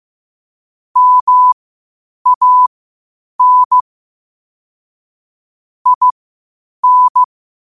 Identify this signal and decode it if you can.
Morse code, heard as a keyed tone, about 15 wpm, spaced as for 8 wpm, 1000 Hz. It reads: MAN IN
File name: qotdfiles / 8wpm